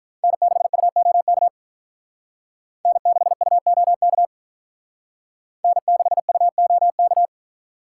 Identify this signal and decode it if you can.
Morse code, heard as a keyed tone, about 45 wpm, 700 Hz. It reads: N6UOK N6UOK N6UOK